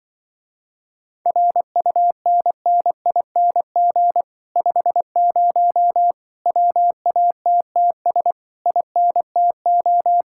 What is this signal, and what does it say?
Morse code, heard as a keyed tone, about 24 wpm, 700 Hz. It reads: RUNNING 50 WATTS INTO
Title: Speed-Racing Common QSO Phrases - 20wpm